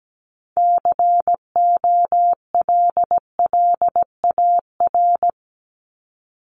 Morse code, heard as a keyed tone, about 17 wpm, 700 Hz. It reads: COLLAR
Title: Top 1751-2000 Words 17wpm